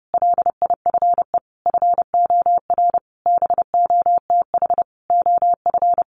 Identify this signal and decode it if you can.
Morse code, heard as a keyed tone, about 30 wpm, 700 Hz. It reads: LIFE FOR BOTH OF
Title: Sentences from Top 1100 Words 30wpm